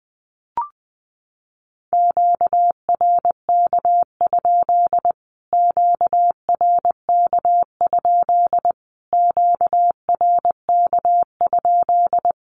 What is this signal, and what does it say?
Morse code, heard as a keyed tone, about 19 wpm, 700 Hz. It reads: QRK? QRK? QRK?
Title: Q-Codes Repeated 3x 20wpm